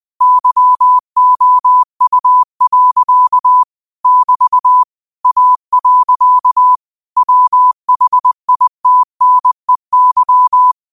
Morse code, heard as a keyed tone, about 20 wpm, 1000 Hz. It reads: YOU. <BT> A. WHITNEY
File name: qotdfiles / 20wpm